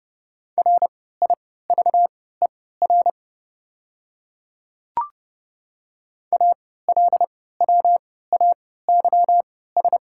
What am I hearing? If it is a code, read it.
Morse code, heard as a keyed tone, about 30 wpm, spaced as for 17 wpm, 700 Hz. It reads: RIVER ALWAYS
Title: 3x Character Spacing - Top 300 Words - 30wpm